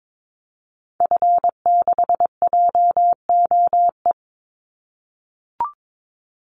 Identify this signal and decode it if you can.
Morse code, heard as a keyed tone, about 22 wpm, 700 Hz. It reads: F6JOE
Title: Call Signs - 22wpm